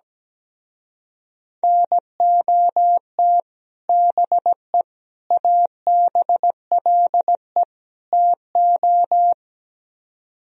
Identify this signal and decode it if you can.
Morse code, heard as a keyed tone, about 17 wpm, 700 Hz. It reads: NOT BE ABLE TO
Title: Speed-Racing - Sets of 4 Words 17wpm